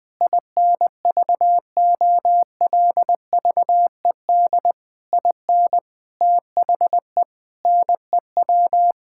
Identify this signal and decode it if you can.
Morse code, heard as a keyed tone, about 20 wpm, 700 Hz. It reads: INVOLVED IN THE NEW
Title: Sets of 8 Words - 20wpm